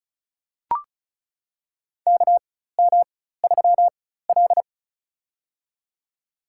Morse code, heard as a keyed tone, about 34 wpm, spaced as for 16 wpm, 700 Hz. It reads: KM3L
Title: USA Call Signs - 4 Characters - 4x Character Spacing - 35wpm